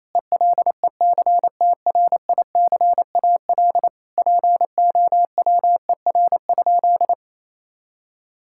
Morse code, heard as a keyed tone, about 28 wpm, 700 Hz. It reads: ELECTRICAL POWER?
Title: Sentences from Top 1000 Words - Review Part 6 - 28wpm